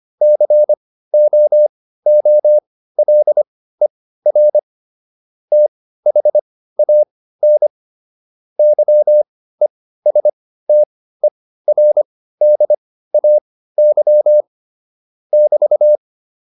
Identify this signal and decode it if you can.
Morse code, heard as a keyed tone, about 25 wpm, spaced as for 15 wpm, 600 Hz. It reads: COOLER THAN YESTERDAY <BT>
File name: QSO409 15